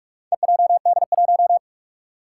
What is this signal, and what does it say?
Morse code, heard as a keyed tone, about 41 wpm, 700 Hz. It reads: EJD1
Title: Four Challenging Letter-Number Combinations 45wpm